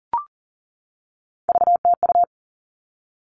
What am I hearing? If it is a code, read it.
Morse code, heard as a keyed tone, about 40 wpm, 700 Hz. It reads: VTU
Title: All Cut Numbers - 3-Digit 40wpm